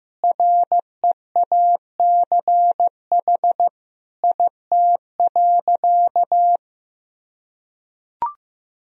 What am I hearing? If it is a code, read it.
Morse code, heard as a keyed tone, about 15 wpm, 700 Hz. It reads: REACH IT.
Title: Sentences from Top 400 Words and Abbreviations 15wpm